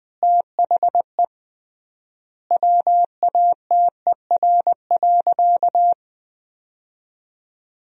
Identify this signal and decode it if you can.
Morse code, heard as a keyed tone, about 21 wpm, 700 Hz. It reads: THE WATER.
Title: 3x Word-Spacing - Sentences from Top 300 Words - 20wpm